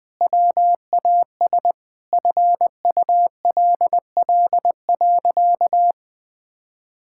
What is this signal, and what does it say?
Morse code, heard as a keyed tone, about 20 wpm, 700 Hz. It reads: WAS FULL.
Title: Sentences from Top 1600 Words 20wpm